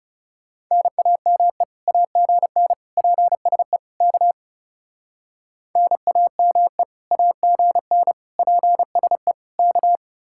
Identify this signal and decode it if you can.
Morse code, heard as a keyed tone, about 38 wpm, 700 Hz. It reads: NAME AGN PSE K NAME AGN PSE K